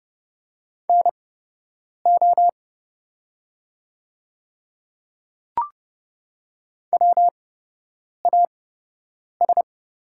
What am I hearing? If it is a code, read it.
Morse code, heard as a keyed tone, about 29 wpm, spaced as for 8 wpm, 700 Hz. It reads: NO WAS
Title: 8x Character Spacing - Top 200 Words - 30wpm